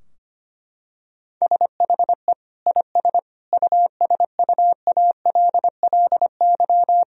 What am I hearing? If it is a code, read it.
Morse code, heard as a keyed tone, about 25 wpm, 700 Hz. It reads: SHE IS USUALLY
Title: Sentences from Top 1000 Words - Review Part 3 - 25wpm